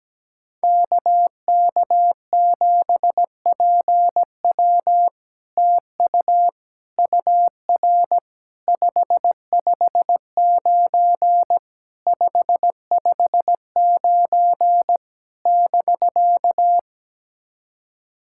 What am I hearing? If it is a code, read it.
Morse code, heard as a keyed tone, about 17 wpm, 700 Hz. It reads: KK7PW TU UR 559 559 <BK>
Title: SOTA QSO Practice 17wpm